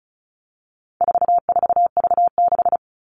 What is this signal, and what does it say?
Morse code, heard as a keyed tone, about 34 wpm, 700 Hz. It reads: 44V6
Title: All Cut Numbers - 4-Digit 35wpm